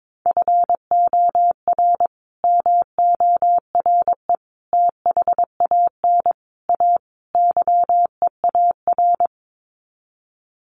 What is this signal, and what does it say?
Morse code, heard as a keyed tone, about 22 wpm, 700 Hz. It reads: FOR MORE THAN A YEAR
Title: Sets of 5 Words 22wpm